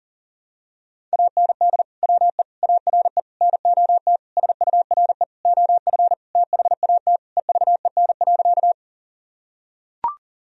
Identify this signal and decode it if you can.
Morse code, heard as a keyed tone, about 40 wpm, 700 Hz. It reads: AND WE ARE NOT SURE OF THAT EVEN.
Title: Sentences from Top 1000 Words - Review Part 1 - 40wpm